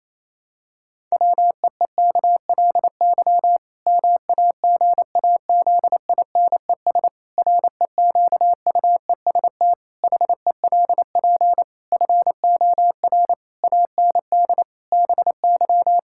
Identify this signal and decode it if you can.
Morse code, heard as a keyed tone, about 27 wpm, 700 Hz. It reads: WEEKLY MAGAZINES REQUEST HELP FOR AND BY
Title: Pangram Challenge v2 - 28wpm